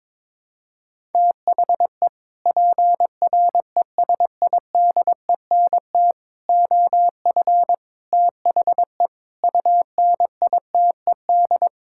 Morse code, heard as a keyed tone, about 22 wpm, 700 Hz. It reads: THE PRESIDENT OF THE UNITED
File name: practice / 22wpm / Sets of 6 Words - 22wpm